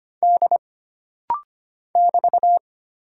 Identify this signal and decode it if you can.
Morse code, heard as a keyed tone, about 25 wpm, 700 Hz. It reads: D <BT>
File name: Single Character (Letter:Number:Punctuation:Prosign) - Rapid-Fire - 25wpm